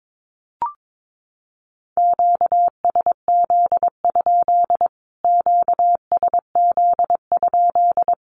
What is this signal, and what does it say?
Morse code, heard as a keyed tone, about 22 wpm, 700 Hz. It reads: QSZ? QSZ?
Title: Q-Codes Repeated 3x 22wpm